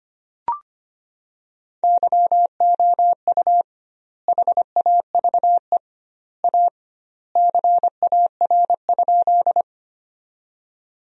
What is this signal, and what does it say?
Morse code, heard as a keyed tone, about 25 wpm, 700 Hz. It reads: YOU HAVE A CAR?
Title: 2x Word-Spacing - Sentences from Top 300 Words - 25wpm